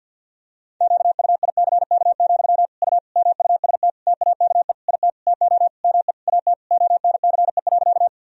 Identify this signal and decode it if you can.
Morse code, heard as a keyed tone, about 52 wpm, 700 Hz. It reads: QUICK, U MUST TAKE IT TO ME AT ONCE.
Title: Sentences from Top 500 Words and Abbreviations 50wpm